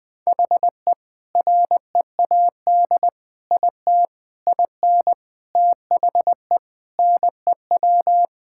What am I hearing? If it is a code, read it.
Morse code, heard as a keyed tone, about 21 wpm, 700 Hz. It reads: HE READ IT IN THE NEW
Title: Sentences with 4-Word Phrases - 20wpm